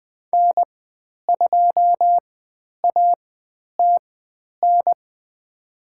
Morse code, heard as a keyed tone, about 20 wpm, spaced as for 10 wpm, 700 Hz. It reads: N2ATN
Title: 109 Lesson 18g - Letter m - Call Signs 20wpm